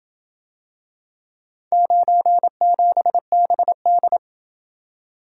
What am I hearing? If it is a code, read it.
Morse code, heard as a keyed tone, about 27 wpm, 700 Hz. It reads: 97BD